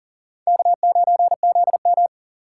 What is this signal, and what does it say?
Morse code, heard as a keyed tone, about 40 wpm, 700 Hz. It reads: K9ZM
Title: USA Call Signs - 4 Characters - 4x Character Spacing - 40wpm